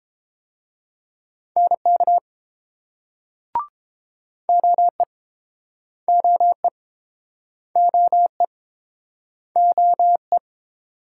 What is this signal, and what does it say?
Morse code, heard as a keyed tone, about 30 wpm, 700 Hz. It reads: NK OE OE OE OE